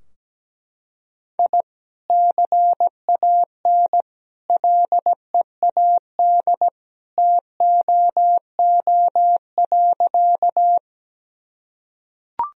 Morse code, heard as a keyed tone, about 17 wpm, 700 Hz. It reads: I CAN LEAD TOO.
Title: Sentences from Top 1000 Words - Review Part 2 - 17wpm